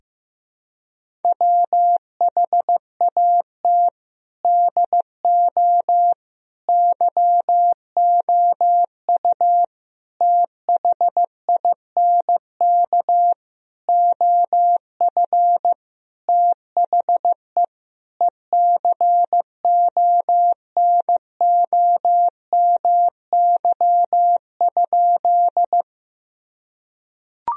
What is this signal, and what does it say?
Morse code, heard as a keyed tone, about 15 wpm, 700 Hz. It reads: WHAT DO YOU THINK OF THE ECONOMY?
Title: Sentences from Top 1000 Words - Review Part 3 - 15wpm